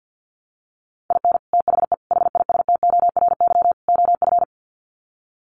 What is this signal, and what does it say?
Morse code, heard as a keyed tone, about 48 wpm, 700 Hz. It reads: IN THE HISTORY OF
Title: Sets of 4 Words 50wpm